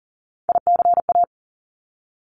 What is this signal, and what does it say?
Morse code, heard as a keyed tone, about 40 wpm, 700 Hz. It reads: ICA
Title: Top Three Letter Combinations in Words 40wpm